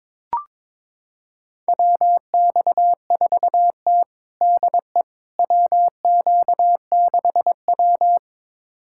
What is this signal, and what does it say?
Morse code, heard as a keyed tone, about 21 wpm, 700 Hz. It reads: WX4T DE WQ6W
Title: USA Callsign DE Callsign - 4 Characters - 22wpm